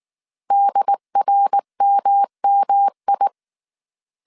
Morse code, heard as a keyed tone, about 19 wpm, 800 Hz. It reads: DRMMI